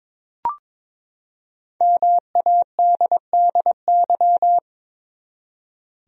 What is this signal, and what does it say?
Morse code, heard as a keyed tone, about 21 wpm, 700 Hz. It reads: MADDY
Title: USA Common Names - 22wpm